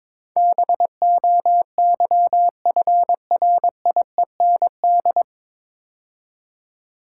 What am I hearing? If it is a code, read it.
Morse code, heard as a keyed tone, about 22 wpm, 700 Hz. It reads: BOYFRIEND
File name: 9 Letter Words - First 3 Letters Given - 22wpm